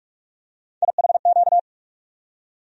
Morse code, heard as a keyed tone, about 41 wpm, 700 Hz. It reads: IHQ
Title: USA Call Sign Postfixes - 3 Letters 45wpm